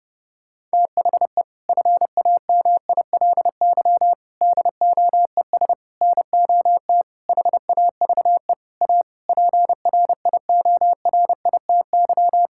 Morse code, heard as a keyed tone, about 30 wpm, 700 Hz. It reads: THE FAMILY DOES NOT HAVE A PRIORITY
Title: Sentences from Top 1600 Words 30wpm